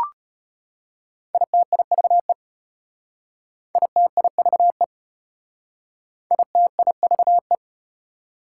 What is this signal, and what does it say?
Morse code, heard as a keyed tone, about 31 wpm, 700 Hz. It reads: ITIVE ITIVE ITIVE